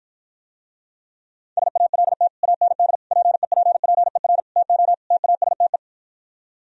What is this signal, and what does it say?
Morse code, heard as a keyed tone, about 53 wpm, 700 Hz. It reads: SALT AND PEPPER TO TASTE